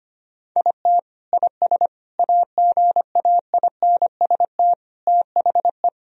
Morse code, heard as a keyed tone, about 25 wpm, 700 Hz. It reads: IT IS AGAINST THE
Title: Sentences from Top 1000 Words - Review Part 3 - 25wpm